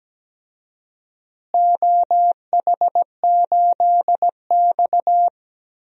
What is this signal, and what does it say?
Morse code, heard as a keyed tone, about 17 wpm, 700 Hz. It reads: OH8X
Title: DXCC-100 Call Signs - 4 Characters - Repeated 2x - 17wpm